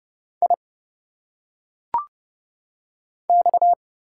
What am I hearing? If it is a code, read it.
Morse code, heard as a keyed tone, about 30 wpm, 700 Hz. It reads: I X
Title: Single Letters - Phonetic Pronunciation 30wpm